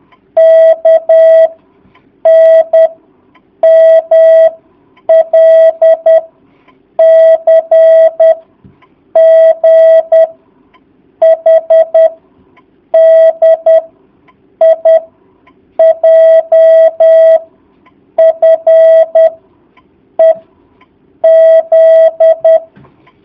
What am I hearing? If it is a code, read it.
Morse code, heard as a keyed tone, about 10 wpm, 650 Hz. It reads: K N M L C G H D I J F E Z
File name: jh8cbh / 2019cw